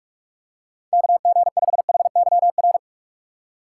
Morse code, heard as a keyed tone, about 46 wpm, 700 Hz. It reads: KG5HYR